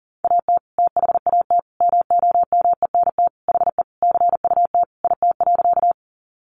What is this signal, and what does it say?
Morse code, heard as a keyed tone, about 40 wpm, 700 Hz. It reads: AT THAT MOMENT HE CUT IT.
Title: Sentences from Top 1000 Words 40wpm